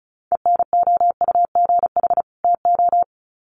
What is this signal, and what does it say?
Morse code, heard as a keyed tone, about 34 wpm, 700 Hz. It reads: ENOUGH TO
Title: Sets of 2 Words 35wpm